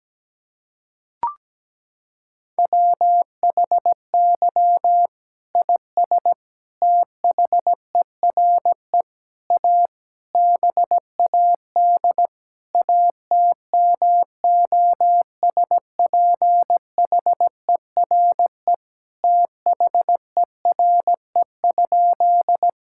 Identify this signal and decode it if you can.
Morse code, heard as a keyed tone, about 17 wpm, 700 Hz. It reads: WHY IS THERE A BAD ATMOSPHERE THERE?
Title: Sentences from Top 1600 Words 17wpm